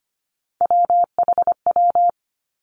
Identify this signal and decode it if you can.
Morse code, heard as a keyed tone, about 25 wpm, 700 Hz. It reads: WHW
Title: Three Challenging Letter-Number Combinations 25wpm